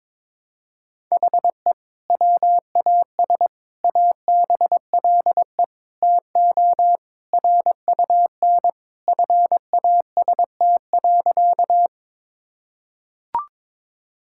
Morse code, heard as a keyed tone, about 21 wpm, 700 Hz. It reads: HE WAS ABLE TO RUN FAST.
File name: practice / 22wpm / Sentences with 2-Word Phrases - 22wpm